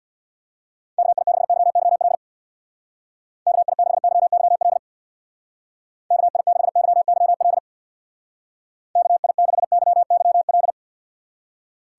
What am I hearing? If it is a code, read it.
Morse code, heard as a keyed tone, about 64 wpm, 700 Hz. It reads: KI6YYL KI6YYL KI6YYL KI6YYL